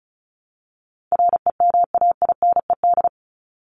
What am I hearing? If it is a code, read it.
Morse code, heard as a keyed tone, about 34 wpm, 700 Hz. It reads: REMAINED
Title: ED Suffix - Top 100 Words with Base Words 35wpm